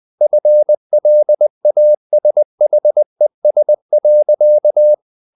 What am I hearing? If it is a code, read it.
Morse code, heard as a keyed tone, about 20 wpm, 600 Hz. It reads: FLASHES.
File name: the old man and the sea part 2 20wpm